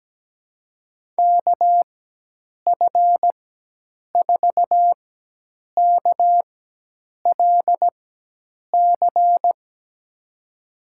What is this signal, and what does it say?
Morse code, heard as a keyed tone, about 17 wpm, spaced as for 8 wpm, 700 Hz. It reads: KF4KLC
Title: USA Call Signs - 6 Characters - 4x Character Spacing - 17wpm